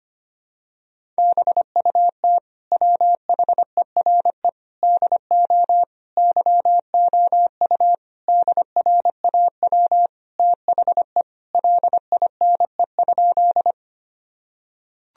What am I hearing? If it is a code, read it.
Morse code, heard as a keyed tone, about 25 wpm, 700 Hz. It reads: BUT WHERE DO YOU DRAW THE LINE?
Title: Sentences from Top 1000 Words - Review Part 1 - 25wpm